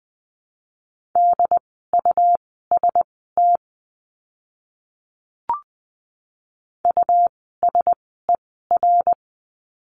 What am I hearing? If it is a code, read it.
Morse code, heard as a keyed tone, about 20 wpm, spaced as for 14 wpm, 700 Hz. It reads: DUST USER
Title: 101 Lesson 17f - Letter u - Words 20wpm